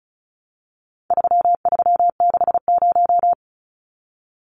Morse code, heard as a keyed tone, about 34 wpm, 700 Hz. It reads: 3360
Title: Numbers - 4-Digit 35wpm